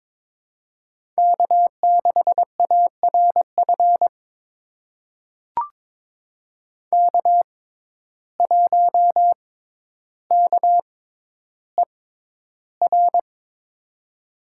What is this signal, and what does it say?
Morse code, heard as a keyed tone, about 22 wpm, spaced as for 11 wpm, 700 Hz. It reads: K6ARF K1KER
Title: USA Call Signs - 5 Characters - 6x Character Spacing - 22wpm